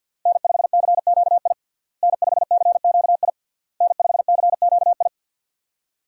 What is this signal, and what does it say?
Morse code, heard as a keyed tone, about 46 wpm, 700 Hz. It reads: N5CQI N5CQI N5CQI
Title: USA Call Signs - 5 Characters - Repeated 3x - 50wpm